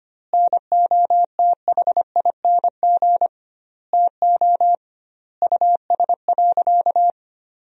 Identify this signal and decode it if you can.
Morse code, heard as a keyed tone, about 25 wpm, 700 Hz. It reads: NOTHING TO US.
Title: 2x Word-Spacing - Sentences from Top 500 Words - 25wpm